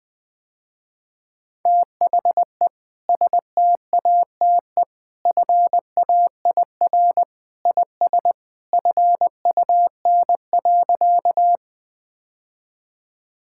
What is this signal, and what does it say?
Morse code, heard as a keyed tone, about 20 wpm, 700 Hz. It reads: THE STATE FAIR IS FUN.
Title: Sentences with 2-Word Phrases - Spoken Beginning - 20wpm